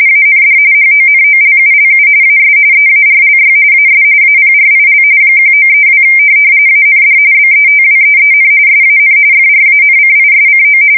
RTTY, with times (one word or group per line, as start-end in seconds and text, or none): 0.1-0.6 AND
0.7-2.1 HONOLULU
2.2-2.6 AS
2.7-3.5 EARLY
3.7-4.0 AS
4.2-5.0 APRIL
5.4-6.0 1932
6.7-7.2 AND
7.3-8.5 BETWEEN
8.7-9.2 SAN
9.3-10.8 FRANCISCO